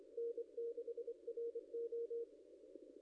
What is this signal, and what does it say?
Morse code, heard as a keyed tone, about 25 wpm, 470 Hz. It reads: N6RO
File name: contests / ss-cw-k5tr-op-n5rz -0900